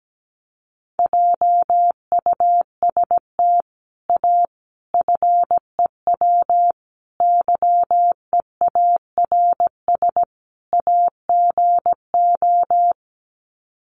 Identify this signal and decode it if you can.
Morse code, heard as a keyed tone, about 17 wpm, 700 Hz. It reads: JUST A FEW YEARS AGO
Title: Sets of 5 Words Encore 17wpm